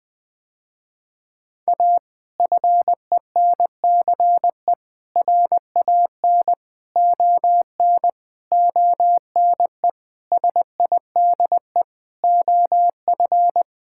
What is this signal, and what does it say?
Morse code, heard as a keyed tone, about 21 wpm, 700 Hz. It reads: A FENCE RAN ON ONE SIDE OF
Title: Sentences with 5-Word Phrases - Spoken Beginning - 20wpm